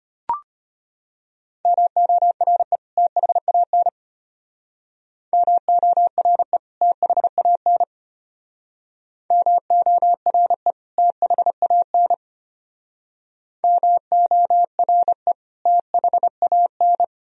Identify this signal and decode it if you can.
Morse code, heard as a keyed tone, about 32 wpm, 700 Hz. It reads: MORE THAN MORE THAN MORE THAN MORE THAN